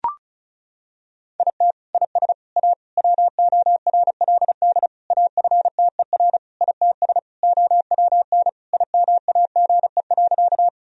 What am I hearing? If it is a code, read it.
Morse code, heard as a keyed tone, about 36 wpm, 700 Hz. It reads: IT IS A WORLD AFTER ITS OWN IMAGE.
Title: Sentences from Top 1000 Words - Review Part 4 - 35wpm